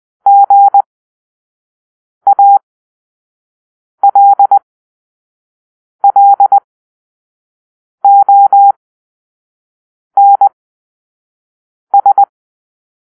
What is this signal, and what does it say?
Morse code, heard as a keyed tone, about 20 wpm, spaced as for 5 wpm, 800 Hz. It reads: GALLONS
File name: Grp-3 Drill-1